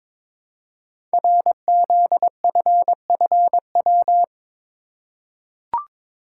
Morse code, heard as a keyed tone, about 22 wpm, 700 Hz. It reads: RZFFW
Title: Code Groups - 5 Letters - 22wpm